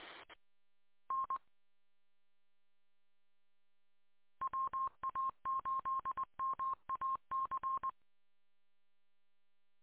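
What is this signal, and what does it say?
Morse code, heard as a keyed tone, about 20 wpm, 1060 Hz. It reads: N WA8MAC
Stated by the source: Morse code recorded from an audio archive on 147.200 MHz.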